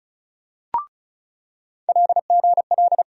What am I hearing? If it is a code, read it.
Morse code, heard as a keyed tone, about 36 wpm, 700 Hz. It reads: LGL
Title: USA Call Sign Postfixes - 3 Letters 35wpm